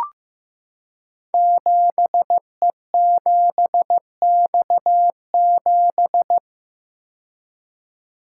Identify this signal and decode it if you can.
Morse code, heard as a keyed tone, about 15 wpm, 700 Hz. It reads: 7E7X7
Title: Code Groups - 5 Letter:Numbers - 15wpm